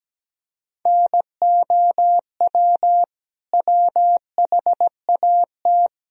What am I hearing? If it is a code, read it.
Morse code, heard as a keyed tone, about 17 wpm, 700 Hz. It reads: NOW WHAT
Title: Sentences from Top 1000 Words - Review Part 1 - 17wpm